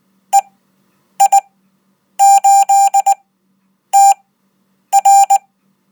Morse code, heard as a keyed tone, about 19 wpm, spaced as for 8 wpm, 790 Hz. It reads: EI8TR
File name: Code